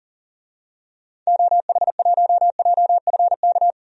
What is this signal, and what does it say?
Morse code, heard as a keyed tone, about 40 wpm, 700 Hz. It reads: OH1JFK
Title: DXCC-100 Call Signs - 6 Characters - 40wpm